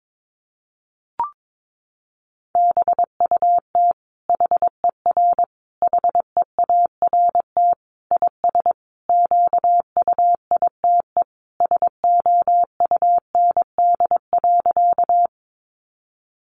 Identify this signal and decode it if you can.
Morse code, heard as a keyed tone, about 22 wpm, 700 Hz. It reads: BUT HER HEART IS QUITE SOUND.
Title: Sentences from Top 1200 Words 22wpm